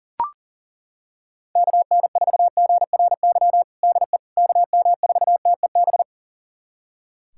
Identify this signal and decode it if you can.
Morse code, heard as a keyed tone, about 37 wpm, 700 Hz. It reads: KN4GRY DE KM4TEB
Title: USA Callsign DE Callsign 40wpm